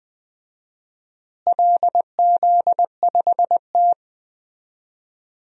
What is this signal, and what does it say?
Morse code, heard as a keyed tone, about 20 wpm, 700 Hz. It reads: LZ5T
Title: DXCC-100 Call Signs - 4 Characters - Repeated 2x - 20wpm